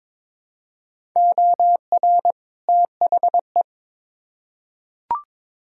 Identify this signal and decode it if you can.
Morse code, heard as a keyed tone, about 22 wpm, 700 Hz. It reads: OR THE
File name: Build A Phrase - 2-Word Phrases - 22wpm